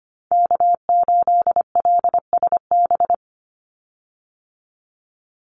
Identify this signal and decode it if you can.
Morse code, heard as a keyed tone, about 25 wpm, 700 Hz. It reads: K8LSB
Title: Call Signs Repeated 3x - 25wpm